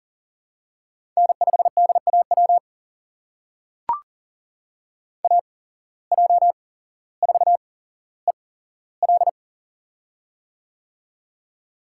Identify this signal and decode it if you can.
Morse code, heard as a keyed tone, about 40 wpm, spaced as for 17 wpm, 700 Hz. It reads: N5DAW AJ4EL
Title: USA Call Signs - 5 Characters - 8x Character Spacing - 40wpm